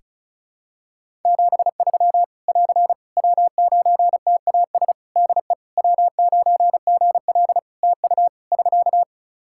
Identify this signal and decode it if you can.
Morse code, heard as a keyed tone, about 36 wpm, 700 Hz. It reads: 73 <AR> W9TAS DE W9GL TU <SK>
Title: Practice QSO - Exchange 4 - Ending - 35wpm